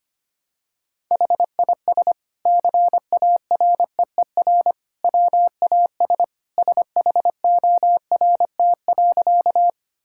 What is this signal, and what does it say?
Morse code, heard as a keyed tone, about 25 wpm, 700 Hz. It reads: HIS CAREER WAS SHORT.
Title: Sentences from Top 1000 Words - Review Part 6 - 25wpm